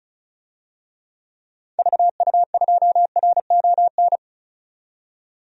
Morse code, heard as a keyed tone, about 36 wpm, 700 Hz. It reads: VU2RON
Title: DXCC-100 Call Signs - 6 Characters - 35wpm